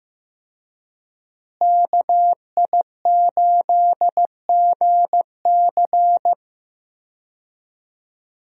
Morse code, heard as a keyed tone, about 16 wpm, 700 Hz. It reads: KI8GC
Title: USA Call Signs - 5 Characters - Repeated 3x - 15wpm